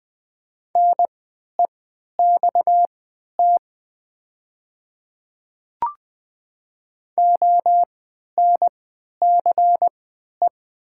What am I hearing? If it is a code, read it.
Morse code, heard as a keyed tone, about 20 wpm, spaced as for 11 wpm, 700 Hz. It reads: NEXT ONCE
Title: 3x Character Spacing - Top 500 Words - 20wpm